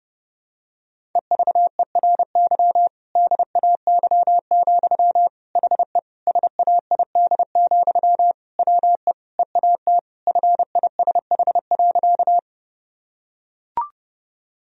Morse code, heard as a keyed tone, about 30 wpm, 700 Hz. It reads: EVERY DAY, HE SAID, WE EAT FISH.
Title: Sentences from Top 1000 Words - Review Part 1 - 30wpm